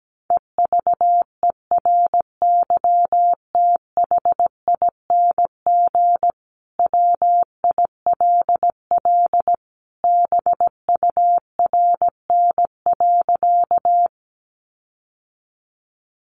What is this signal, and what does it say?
Morse code, heard as a keyed tone, about 17 wpm, 700 Hz. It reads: EVERYTHING WILL BURN.
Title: Sentences from Top 1300 Words 17wpm